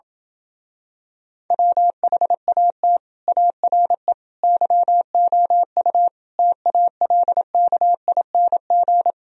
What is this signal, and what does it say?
Morse code, heard as a keyed tone, about 27 wpm, 700 Hz. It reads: WHAT ARE YOU TALKING